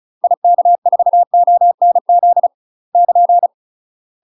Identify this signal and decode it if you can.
Morse code, heard as a keyed tone, about 34 wpm, 700 Hz. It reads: IK4ONZ <KN>
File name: UK MASTER QSOs TRAINING / 35WPM / UK MASTER QSOs 0040